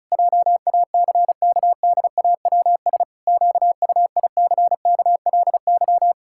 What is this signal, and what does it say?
Morse code, heard as a keyed tone, about 36 wpm, 700 Hz. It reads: JACKDAWS QUICKLY
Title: Pangram Challenge v2 - 35wpm